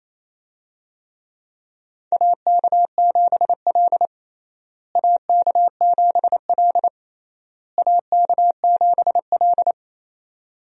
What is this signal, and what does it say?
Morse code, heard as a keyed tone, about 27 wpm, 700 Hz. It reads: AK7L AK7L AK7L
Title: USA Call Signs - 4 Characters - Repeated 3x - 28wpm